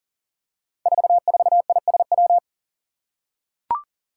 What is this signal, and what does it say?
Morse code, heard as a keyed tone, about 39 wpm, 700 Hz. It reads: 44ISW
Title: Five Challenging Letter-Number Combinations 40wpm